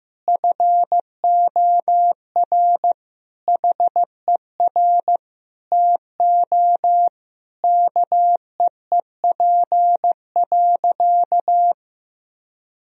Morse code, heard as a keyed tone, about 15 wpm, 700 Hz. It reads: FOR HER TO KEEP.
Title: Sentences from Top 1000 Words - Review Part 2 - 15wpm